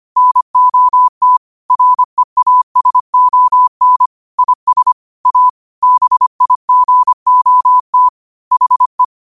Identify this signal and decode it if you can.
Morse code, heard as a keyed tone, about 25 wpm, 1000 Hz. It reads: NOT REASON IS A BIGOT HE